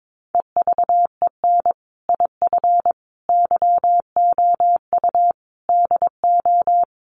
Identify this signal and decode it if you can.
Morse code, heard as a keyed tone, about 22 wpm, 700 Hz. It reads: EVEN IF YOU DO
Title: Sets of 5 Words Encore 22wpm